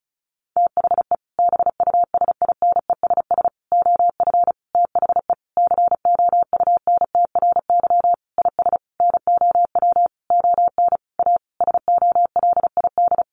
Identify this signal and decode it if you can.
Morse code, heard as a keyed tone, about 34 wpm, 700 Hz. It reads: THE BUSINESS OF THE COUNTRY IS NOW ON A SOLID
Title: Sentences from Top 1500 Words 35wpm